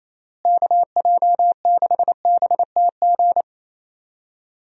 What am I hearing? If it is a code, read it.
Morse code, heard as a keyed tone, about 28 wpm, 700 Hz. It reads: KJ6BTG
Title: USA Call Signs - 6 Characters - Repeated 3x - 28wpm